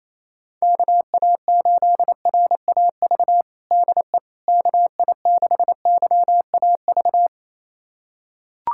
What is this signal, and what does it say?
Morse code, heard as a keyed tone, about 28 wpm, 700 Hz. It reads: KA8RAV DE KI6YAV
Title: USA Callsign DE Callsign - 6 Characters - 28wpm